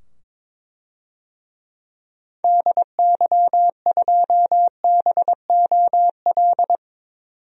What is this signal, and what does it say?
Morse code, heard as a keyed tone, about 22 wpm, 700 Hz. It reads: DY2BOL
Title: DXCC-100 Call Signs - 6 Characters - 22wpm